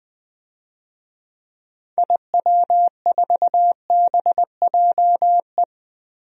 Morse code, heard as a keyed tone, about 20 wpm, 700 Hz. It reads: IW4BJE
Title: DXCC-100 Call Signs - 6 Characters - Repeated 2x - 20wpm